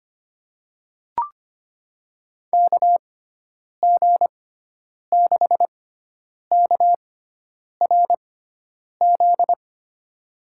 Morse code, heard as a keyed tone, about 25 wpm, spaced as for 9 wpm, 700 Hz. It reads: KG6KRZ
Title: USA Call Signs - 6 Characters - 6x Character Spacing - 25wpm